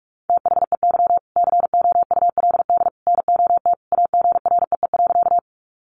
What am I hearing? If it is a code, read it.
Morse code, heard as a keyed tone, about 44 wpm, 700 Hz. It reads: THEY COULD NOT AGREE.
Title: Sentences from Top 1100 Words 45wpm